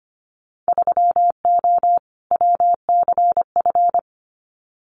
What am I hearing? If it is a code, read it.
Morse code, heard as a keyed tone, about 25 wpm, 700 Hz. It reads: 3O WCF
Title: Winter Field Day Exchanges Encore 25wpm